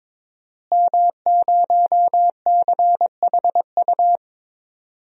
Morse code, heard as a keyed tone, about 22 wpm, 700 Hz. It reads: M0CHU
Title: DXCC-100 Call Signs - 5 Characters - Repeated 2x - 22wpm